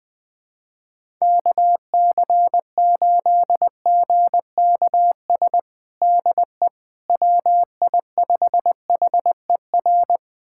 Morse code, heard as a keyed tone, about 20 wpm, 700 Hz. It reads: KC8GKS DE WI5HER
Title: USA Callsign DE Callsign 20wpm